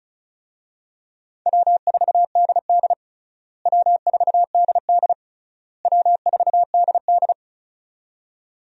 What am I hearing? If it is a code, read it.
Morse code, heard as a keyed tone, about 35 wpm, 700 Hz. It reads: W4DD W4DD W4DD
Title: USA Call Signs - 4 Characters - Repeated 3x - 35wpm